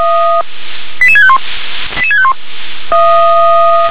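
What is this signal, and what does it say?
Five-tone selective call (ZVEI), heard as quick runs of five tones, tones of about 70 ms. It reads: 88651 88651
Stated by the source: ZVEI five-tone selective call received by SDR.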